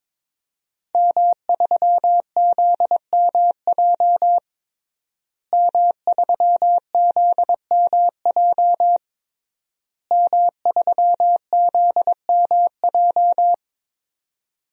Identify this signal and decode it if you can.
Morse code, heard as a keyed tone, about 22 wpm, 700 Hz. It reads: M3ZMJ M3ZMJ M3ZMJ
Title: DXCC-100 Call Signs - 5 Characters - Repeated 3x - 22wpm